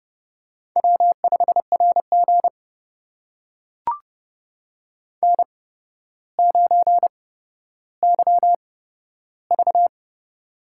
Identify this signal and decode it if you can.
Morse code, heard as a keyed tone, about 30 wpm, spaced as for 13 wpm, 700 Hz. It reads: W5RG N9YV
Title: USA Call Signs - 4 Characters - 8x Character Spacing - 30wpm